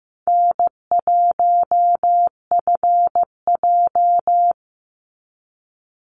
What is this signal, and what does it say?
Morse code, heard as a keyed tone, about 15 wpm, 700 Hz. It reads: N1FJ
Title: NA SOTA Activator Callsigns 15wpm